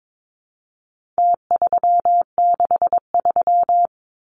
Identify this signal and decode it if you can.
Morse code, heard as a keyed tone, about 22 wpm, 700 Hz. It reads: T363
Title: Numbers - Contest Style - 4-Digit 22wpm